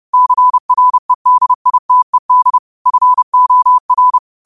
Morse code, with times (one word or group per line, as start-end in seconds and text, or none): 0.1-2.6 CREDITED
2.9-4.2 FOR